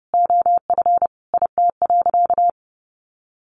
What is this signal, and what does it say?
Morse code, heard as a keyed tone, about 30 wpm, 700 Hz. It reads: OF IT.
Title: Sentences from Top 1400 Words 30wpm